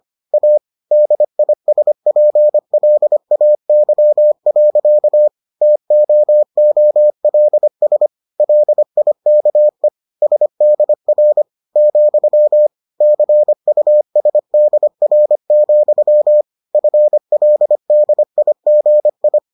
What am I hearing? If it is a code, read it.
Morse code, heard as a keyed tone, about 25 wpm, 600 Hz. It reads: A DISPLAY. TOOLS LIKE SDR , CUSDR, FLDIGI